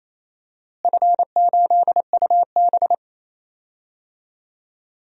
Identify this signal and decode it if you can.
Morse code, heard as a keyed tone, about 28 wpm, 700 Hz. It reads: F8UB
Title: DXCC-100 Call Signs - 4 Characters - Repeated 2x - 28wpm